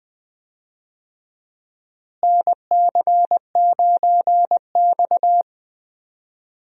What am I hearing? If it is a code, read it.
Morse code, heard as a keyed tone, about 20 wpm, 700 Hz. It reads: NC9X
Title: USA Call Signs - 4 Characters - Repeated 3x - 20wpm